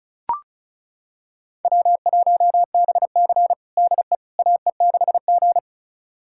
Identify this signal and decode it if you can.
Morse code, heard as a keyed tone, about 36 wpm, 700 Hz. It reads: W1BC DE AE6G
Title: USA Callsign DE Callsign - 4 Characters - 35wpm